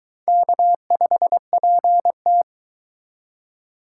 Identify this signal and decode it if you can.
Morse code, heard as a keyed tone, about 22 wpm, 700 Hz. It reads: K5PT